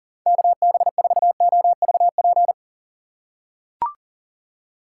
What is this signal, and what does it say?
Morse code, heard as a keyed tone, about 40 wpm, 700 Hz. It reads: KB4OVP
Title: USA Call Signs - 6 Characters - 4x Character Spacing - 40wpm